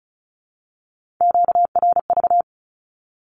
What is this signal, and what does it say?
Morse code, heard as a keyed tone, about 34 wpm, 700 Hz. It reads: QRV
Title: Q-Codes Repeated 3x 35wpm